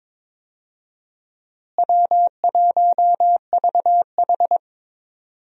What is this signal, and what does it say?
Morse code, heard as a keyed tone, about 22 wpm, 700 Hz. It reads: W1VH
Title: USA Call Signs - 4 Characters - Repeated 3x - 22wpm